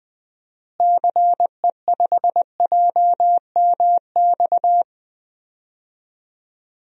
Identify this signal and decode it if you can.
Morse code, heard as a keyed tone, about 20 wpm, 700 Hz. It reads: CE5JMX
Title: DXCC-100 Call Signs - 6 Characters - Repeated 2x - 20wpm